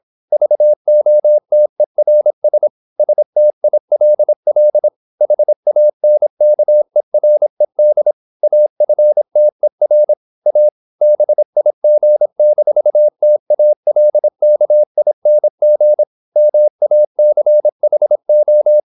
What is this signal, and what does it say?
Morse code, heard as a keyed tone, about 26 wpm, 600 Hz. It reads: VOTERS STILL HANKERED AFTER A BIG-TALKING MACHO
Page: Never 26wpm | CW med Gnister